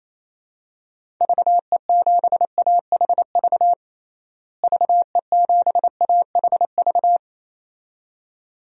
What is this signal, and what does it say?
Morse code, heard as a keyed tone, about 28 wpm, 700 Hz. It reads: VE7AHV VE7AHV
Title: DXCC-100 Call Signs - 6 Characters - Repeated 2x - 28wpm